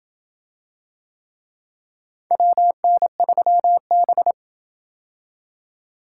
Morse code, heard as a keyed tone, about 27 wpm, 700 Hz. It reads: WN3B